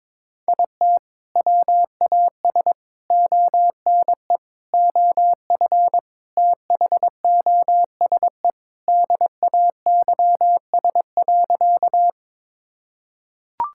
Morse code, heard as a keyed tone, about 22 wpm, 700 Hz. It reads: IT WAS ONE OF THOSE DAYS.
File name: practice / 22wpm / Sentences with 4-Word Phrases - 22wpm